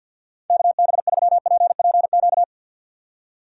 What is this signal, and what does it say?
Morse code, heard as a keyed tone, about 50 wpm, 700 Hz. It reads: KB3PPQ
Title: USA Call Signs - 2x Character Spacing - 50wpm